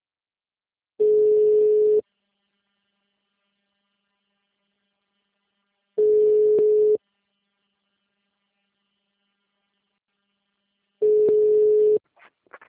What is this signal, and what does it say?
Ringback tone (Europe, 425 Hz).